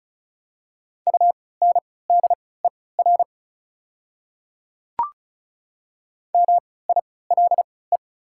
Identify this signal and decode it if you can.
Morse code, heard as a keyed tone, about 36 wpm, spaced as for 20 wpm, 690 Hz. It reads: UNDER MILE
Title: 3x Character Spacing - Top 300 Words - 35wpm